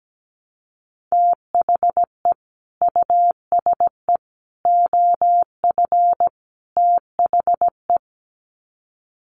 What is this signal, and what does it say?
Morse code, heard as a keyed tone, about 17 wpm, 700 Hz. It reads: THE USE OF THE
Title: Sets of 4 Words 17wpm